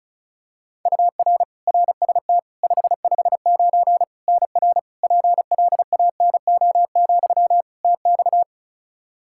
Morse code, heard as a keyed tone, about 36 wpm, 700 Hz. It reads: UR RST 559 NR PLANO, TX
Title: Practice QSO - Exchange 1 - Beginning - 35wpm